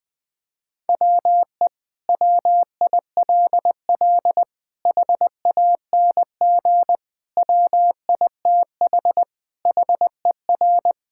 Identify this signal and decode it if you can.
Morse code, heard as a keyed tone, about 20 wpm, 700 Hz. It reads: WE WILL HANG WITH HER
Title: Sentences from Top 1600 Words 20wpm